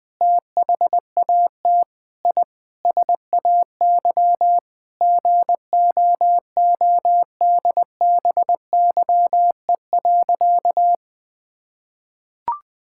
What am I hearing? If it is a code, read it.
Morse code, heard as a keyed tone, about 20 wpm, 700 Hz. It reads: THAT I SAY GOODBYE.
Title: Sentences with 7-Word Phrases - 20wpm